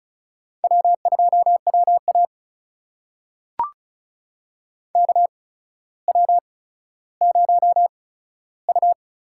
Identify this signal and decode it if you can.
Morse code, heard as a keyed tone, about 36 wpm, spaced as for 15 wpm, 700 Hz. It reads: W2WA KW0U
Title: USA Call Signs - 4 Characters - 8x Character Spacing - 35wpm